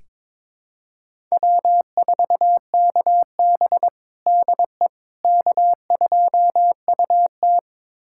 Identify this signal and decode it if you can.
Morse code, heard as a keyed tone, about 22 wpm, 700 Hz. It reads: W4KB DE K2UT
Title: USA Callsign DE Callsign - 4 Characters - 22wpm